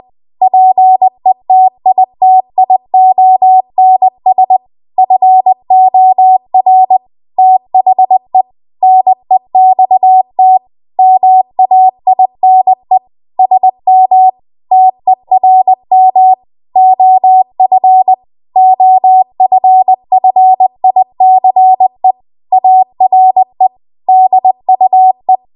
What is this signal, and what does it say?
Morse code, heard as a keyed tone, about 20 wpm, 750 Hz. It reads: PETITIONS FOR THE NEXT MAINE SM TERM OF OFFICE ARE DUE
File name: Archive / 20 WPM / 20WPM